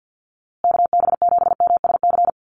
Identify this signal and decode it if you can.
Morse code, heard as a keyed tone, about 50 wpm, 700 Hz. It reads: KB7MSC
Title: USA Call Signs Repeated 2x 50wpm